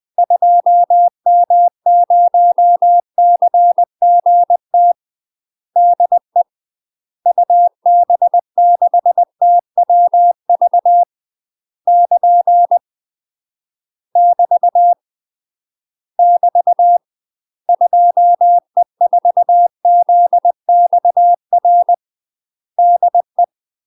Morse code, heard as a keyed tone, about 20 wpm, 700 Hz. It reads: 2M0CGT DE UB6TWV <KN> <BT> <BT> 2E4ZXR DE
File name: UK MASTER QSOs TRAINING / 20WPM / UK MASTER QSOs 0069